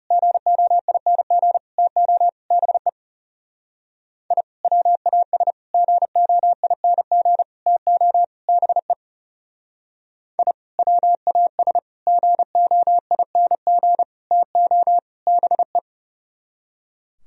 Morse code, heard as a keyed tone, about 36 wpm, 700 Hz. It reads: GOING TO BE I WAS GOING TO BE I WAS GOING TO BE
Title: Speed-Racing - Sets of 5 Words 30wpm